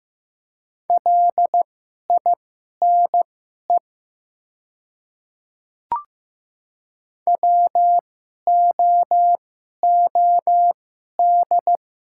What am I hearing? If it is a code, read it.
Morse code, heard as a keyed tone, about 15 wpm, spaced as for 11 wpm, 700 Hz. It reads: LINE WOOD
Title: 2x Character Spacing - Top 300 Words - 15wpm